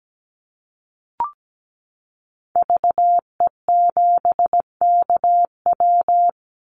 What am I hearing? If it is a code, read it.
Morse code, heard as a keyed tone, about 17 wpm, 700 Hz. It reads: VE7KW
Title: Call Signs - 17wpm